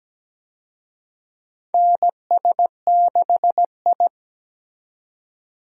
Morse code, heard as a keyed tone, about 17 wpm, 700 Hz. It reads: NS6I
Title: USA Call Signs - 4 Characters - 17wpm